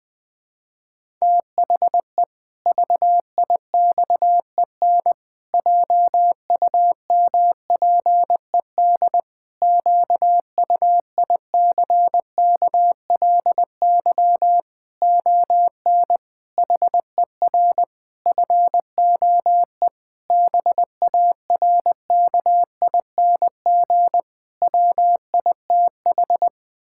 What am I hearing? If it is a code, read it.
Morse code, heard as a keyed tone, about 20 wpm, 700 Hz. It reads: THE VIXEN JUMPED QUICKLY ON HER FOE BARKING WITH
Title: Pangram Challenge v2 - 20wpm